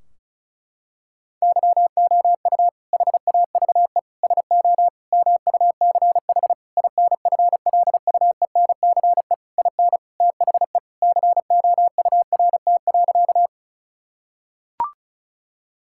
Morse code, heard as a keyed tone, about 34 wpm, 700 Hz. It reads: YOU HAVE SO MUCH INFLUENCE IN THE COURT.
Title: Sentences from Top 1600 Words 35wpm